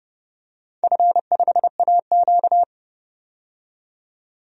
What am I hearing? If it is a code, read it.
Morse code, heard as a keyed tone, about 30 wpm, 700 Hz. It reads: F5AQ
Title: DXCC-100 Call Signs - 4 Characters - 30wpm